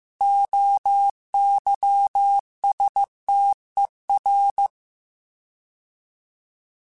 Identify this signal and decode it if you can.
Morse code, heard as a keyed tone, about 15 wpm, 800 Hz. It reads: OYSTER